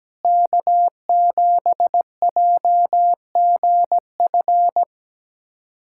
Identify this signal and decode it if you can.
Morse code, heard as a keyed tone, about 17 wpm, 700 Hz. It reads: K7JGF
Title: USA Callsign DE Callsign - 5 Characters - 17wpm